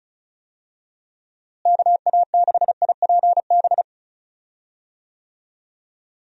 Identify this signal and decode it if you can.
Morse code, heard as a keyed tone, about 36 wpm, 700 Hz. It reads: KA6IPB
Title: USA Call Signs - 6 Characters - Repeated 2x - 35wpm